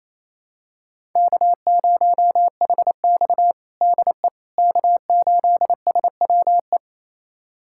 Morse code, heard as a keyed tone, about 28 wpm, 700 Hz. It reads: K0HX DE K8SWE
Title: USA Callsign DE Callsign 28wpm